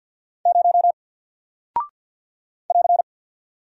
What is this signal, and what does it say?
Morse code, heard as a keyed tone, about 48 wpm, 700 Hz. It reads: 0 <AR>
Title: Single Character (Letter:Number:Punctuation:Prosign) Alt 50wpm